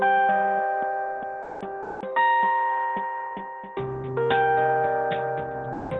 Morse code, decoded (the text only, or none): none